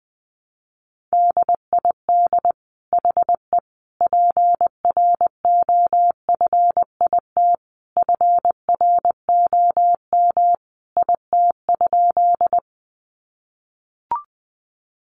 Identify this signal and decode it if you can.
Morse code, heard as a keyed tone, about 20 wpm, 700 Hz. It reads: DID HE PROFIT FROM IT?
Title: Sentences from Top 800 Words 20wpm